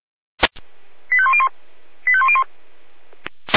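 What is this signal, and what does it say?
Five-tone selective call (ZVEI), heard as quick runs of five tones, tones of about 70 ms. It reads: 85101 85101
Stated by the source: ZVEI five-tone selective call received by SDR.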